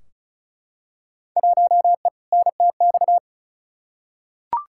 Morse code, heard as a keyed tone, about 34 wpm, 700 Hz. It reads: 1E NTX
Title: Field Day Exchanges - 35wpm